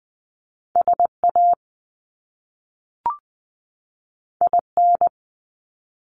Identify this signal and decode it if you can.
Morse code, heard as a keyed tone, about 20 wpm, 700 Hz. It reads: SA IN
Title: Top Two Letter Combinations in Words 20wpm